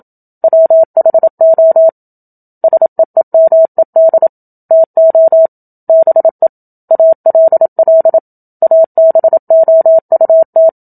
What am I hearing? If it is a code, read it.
Morse code, heard as a keyed tone, about 27 wpm, 650 Hz. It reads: WHO SEEMED TO BE ALL ABOUT
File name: Book Two / 01-Chapter 1 - 27 WPM